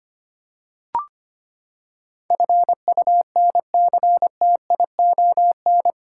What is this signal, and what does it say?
Morse code, heard as a keyed tone, about 25 wpm, 700 Hz. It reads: FUNCTION
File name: Eight Letter Words - 25wpm